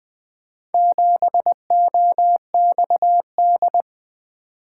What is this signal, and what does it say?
Morse code, heard as a keyed tone, about 20 wpm, 700 Hz. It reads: 7OXD
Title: Four Letter-Number Combinations - 20wpm